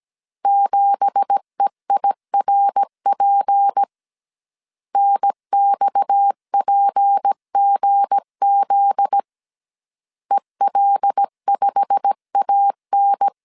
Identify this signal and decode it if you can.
Morse code, heard as a keyed tone, about 17 wpm, 800 Hz. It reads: 7EIRP NXPGZ EL5AN
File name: CW / DET / 1 9 1500C S A 16wpm 2828